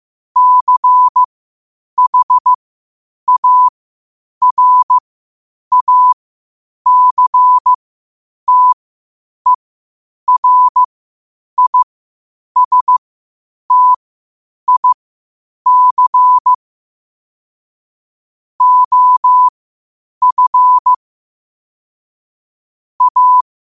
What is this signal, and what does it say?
Morse code, heard as a keyed tone, about 15 wpm, spaced as for 8 wpm, 1000 Hz. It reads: CHARACTERISTIC OF A
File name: qotdfiles / 8wpm